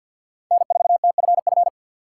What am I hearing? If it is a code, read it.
Morse code, heard as a keyed tone, about 52 wpm, 700 Hz. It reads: N4TFF
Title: USA Call Signs - 4x Character Spacing - 50wpm